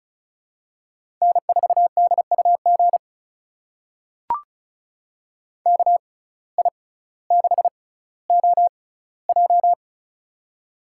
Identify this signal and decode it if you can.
Morse code, heard as a keyed tone, about 36 wpm, spaced as for 18 wpm, 700 Hz. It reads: N4DUG KI6OJ
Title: USA Call Signs - 5 Characters - 6x Character Spacing - 35wpm